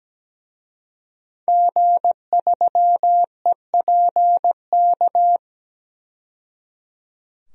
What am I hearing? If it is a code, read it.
Morse code, heard as a keyed tone, about 17 wpm, 700 Hz. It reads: G3EPK
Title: DXCC-100 Call Signs - 5 Characters - 17wpm